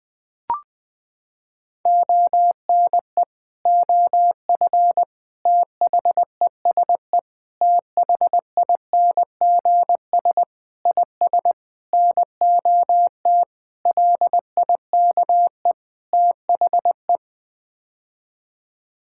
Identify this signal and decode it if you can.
Morse code, heard as a keyed tone, about 20 wpm, 700 Hz. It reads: ONE OF THESE THINGS IS NOT LIKE THE
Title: Sets of 8 Words - 20wpm